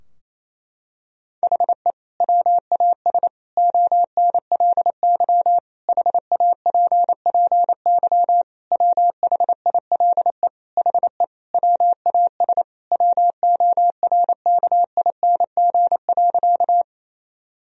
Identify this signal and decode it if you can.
Morse code, heard as a keyed tone, about 27 wpm, 700 Hz. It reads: HE WAS ONLY HAPPY WHILE HE WAS WORKING.
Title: Sentences from Top 1000 Words - Review Part 4 - 28wpm